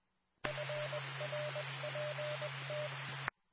Morse code, heard as a keyed tone, about 20 wpm, 600 Hz. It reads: FRPT
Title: Nassau and Suffolk County Amateur Repeaters